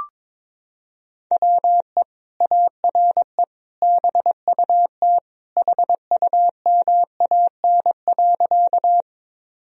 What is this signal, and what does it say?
Morse code, heard as a keyed tone, about 21 wpm, 700 Hz. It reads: WE ARE BUT HUMAN.
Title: Sentences from Top 1000 Words - Review Part 3 - 22wpm